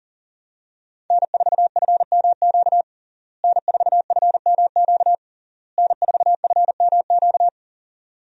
Morse code, heard as a keyed tone, about 40 wpm, 700 Hz. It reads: N4FMQ N4FMQ N4FMQ
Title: USA Call Signs - 5 Characters - Repeated 3x - 40wpm